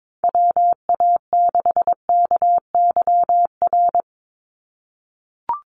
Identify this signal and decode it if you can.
Morse code, heard as a keyed tone, about 22 wpm, 700 Hz. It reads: WA6KYR
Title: NA SOTA Chaser Callsigns 22wpm